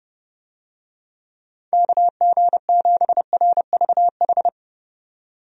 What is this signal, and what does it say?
Morse code, heard as a keyed tone, about 30 wpm, 700 Hz. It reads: KG7RVH
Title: USA Call Signs - 6 Characters - 30wpm